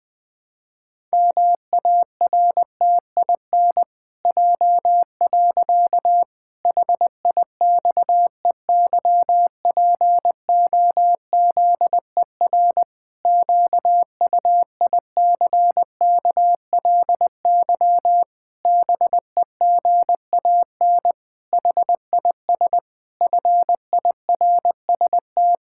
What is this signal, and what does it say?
Morse code, heard as a keyed tone, about 20 wpm, 700 Hz. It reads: MARTIN J. HIXEYPOZER QUICKLY BEGAN HIS FIRST
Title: Pangram Challenge v2 - 20wpm